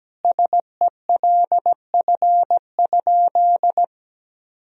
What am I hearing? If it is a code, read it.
Morse code, heard as a keyed tone, about 17 wpm, 700 Hz. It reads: SELF?
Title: Sentences from Top 1000 Words - Review Part 1 - 17wpm